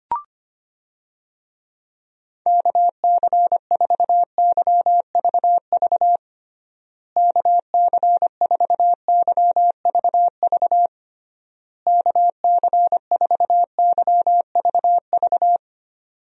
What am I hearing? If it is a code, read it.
Morse code, heard as a keyed tone, about 25 wpm, 700 Hz. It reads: KC4YVV KC4YVV KC4YVV
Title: USA Call Signs - 6 Characters - Repeated 3x - 25wpm